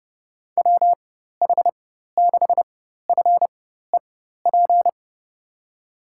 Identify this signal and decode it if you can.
Morse code, heard as a keyed tone, about 30 wpm, spaced as for 14 wpm, 700 Hz. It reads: WH6FEP
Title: USA Call Signs - 4x Character Spacing - 30wpm